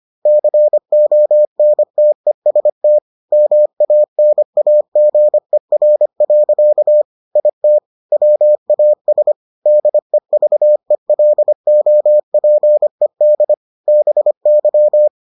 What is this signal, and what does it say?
Morse code, heard as a keyed tone, about 25 wpm, 600 Hz. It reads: CONTEST MANAGER. IT WAS DEVELOPED BY